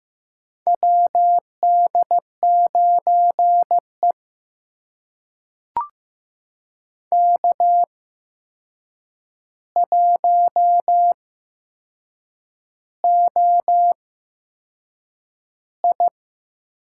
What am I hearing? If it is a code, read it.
Morse code, heard as a keyed tone, about 16 wpm, spaced as for 6 wpm, 700 Hz. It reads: WD9E K1OI
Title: USA Call Signs - 8x Character Spacing - 15wpm